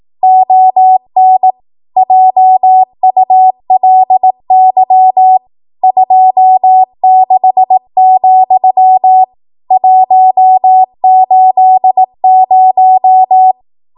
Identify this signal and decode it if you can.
Morse code, heard as a keyed tone, about 18 wpm, 750 Hz. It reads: ON JULY 26, 180
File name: Archive / 18 WPM / 18WPM